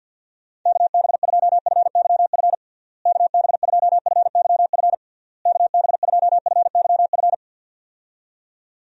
Morse code, heard as a keyed tone, about 52 wpm, 700 Hz. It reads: KB2FYF KB2FYF KB2FYF
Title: USA Call Signs - 6 Characters - Repeated 3x - 50wpm